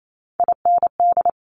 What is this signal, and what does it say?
Morse code, heard as a keyed tone, about 28 wpm, 700 Hz. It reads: IND
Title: Top Three Letter Combinations in Words 28wpm